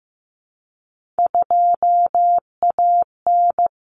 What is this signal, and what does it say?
Morse code, heard as a keyed tone, about 15 wpm, 700 Hz. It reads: 2AN
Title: Common Cut Numbers - 3-Digit 15wpm